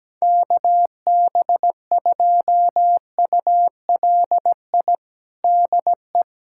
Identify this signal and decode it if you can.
Morse code, heard as a keyed tone, about 17 wpm, 700 Hz. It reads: KB2ULI DE
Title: USA Callsign DE Callsign 17wpm